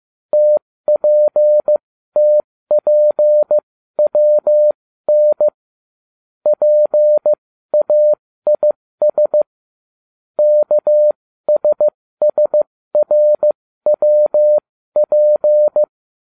Morse code, heard as a keyed tone, about 15 wpm, spaced as for 13 wpm, 600 Hz. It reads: TPTPWN PAIS KSSRWP